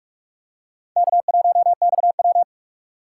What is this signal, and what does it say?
Morse code, heard as a keyed tone, about 44 wpm, 700 Hz. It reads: K1XW